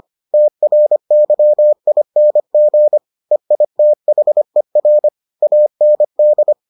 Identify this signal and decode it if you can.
Morse code, heard as a keyed tone, about 25 wpm, 600 Hz. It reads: TRYING EITHER AND